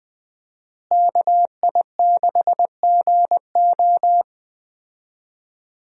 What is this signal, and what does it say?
Morse code, heard as a keyed tone, about 20 wpm, 700 Hz. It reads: KI6GO
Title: USA Call Signs - 5 Characters - Repeated 2x - 20wpm